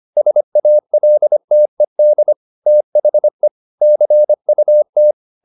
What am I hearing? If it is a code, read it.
Morse code, heard as a keyed tone, about 25 wpm, 600 Hz. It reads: SALTED THE CUT